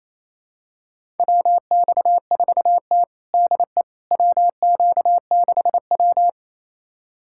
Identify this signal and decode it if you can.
Morse code, heard as a keyed tone, about 28 wpm, 700 Hz. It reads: WX4T DE WQ6W
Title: USA Callsign DE Callsign - 4 Characters - 28wpm